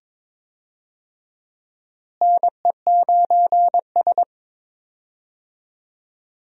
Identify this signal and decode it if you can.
Morse code, heard as a keyed tone, about 22 wpm, 700 Hz. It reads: NE9S
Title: USA Call Signs - 4 Characters - Repeated 2x - 22wpm